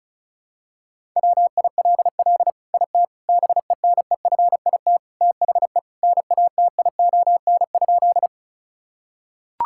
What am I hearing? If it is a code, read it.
Morse code, heard as a keyed tone, about 34 wpm, 700 Hz. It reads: WILL IT BENEFIT THE NATION?
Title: Sentences from Top 1000 Words - Review Part 6 - 35wpm